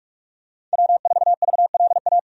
Morse code, heard as a keyed tone, about 42 wpm, 700 Hz. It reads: W4VLA
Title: USA Call Signs - 5 Characters - 45wpm